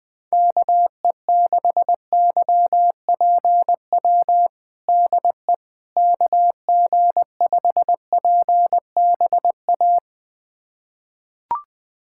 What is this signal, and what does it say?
Morse code, heard as a keyed tone, about 20 wpm, 700 Hz. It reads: KE6YPW DE KG5PBA
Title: USA Callsign DE Callsign - 6 Characters - 20wpm